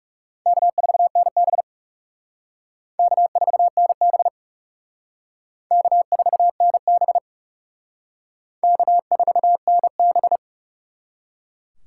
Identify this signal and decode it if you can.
Morse code, heard as a keyed tone, about 35 wpm, 700 Hz. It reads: K4NB K4NB K4NB K4NB
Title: Speed Racing - USA 1x2 Call Signs 30wpm